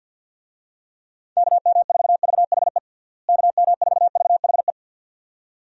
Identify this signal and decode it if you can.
Morse code, heard as a keyed tone, about 47 wpm, 700 Hz. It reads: KM4VHE KM4VHE
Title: USA Call Signs - 6 Characters - Repeated 2x - 50wpm